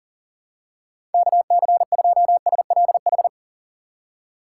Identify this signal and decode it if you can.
Morse code, heard as a keyed tone, about 40 wpm, 700 Hz. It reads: KC2SLH
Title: USA Call Signs - 6 Characters - 6x Character Spacing - 40wpm